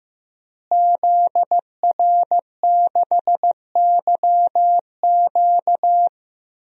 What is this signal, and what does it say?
Morse code, heard as a keyed tone, about 15 wpm, 700 Hz. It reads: ZR6YQ
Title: DXCC-100 Call Signs - 5 Characters - Repeated 3x - 15wpm